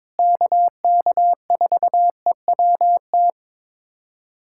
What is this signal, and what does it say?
Morse code, heard as a keyed tone, about 22 wpm, 700 Hz. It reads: KK4EWT
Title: USA Callsign DE Callsign 22wpm